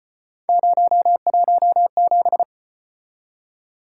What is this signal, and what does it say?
Morse code, heard as a keyed tone, about 36 wpm, 700 Hz. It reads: 017